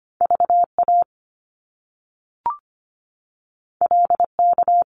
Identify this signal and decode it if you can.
Morse code, heard as a keyed tone, about 25 wpm, 700 Hz. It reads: VA LK
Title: Top Two Letter Combinations in Words 25wpm